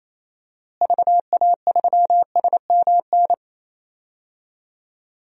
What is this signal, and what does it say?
Morse code, heard as a keyed tone, about 28 wpm, 700 Hz. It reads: VA3SMN
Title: DXCC-100 Call Signs - 6 Characters - Repeated 2x - 28wpm